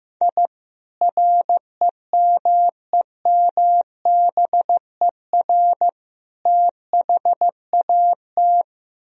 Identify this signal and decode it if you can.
Morse code, heard as a keyed tone, about 15 wpm, 700 Hz. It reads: I REMEMBER THAT
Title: Sentences from Top 1000 Words - Review Part 1 - 15wpm